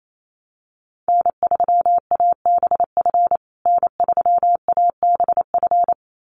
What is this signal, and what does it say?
Morse code, heard as a keyed tone, about 28 wpm, 700 Hz. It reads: N3ABF N3ABF
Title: USA Call Signs Repeated 2x 28wpm